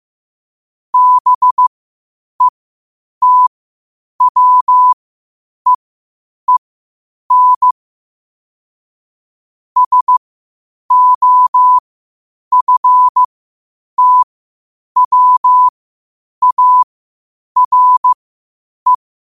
Morse code, heard as a keyed tone, about 15 wpm, spaced as for 8 wpm, 1000 Hz. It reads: BETWEEN SOFTWARE
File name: qotdfiles / 8wpm